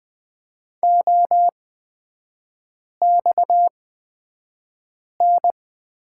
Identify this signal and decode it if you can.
Morse code, heard as a keyed tone, about 20 wpm, spaced as for 5 wpm, 700 Hz. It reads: OXN
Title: 229 Lesson 37e - Letter x - Three Characters 20wpm